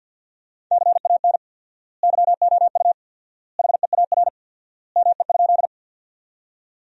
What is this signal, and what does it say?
Morse code, heard as a keyed tone, about 50 wpm, 700 Hz. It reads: CAN YOU HEAR ME?
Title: 4x Word-Spacing - Sentences from Top 300 Words - 50wpm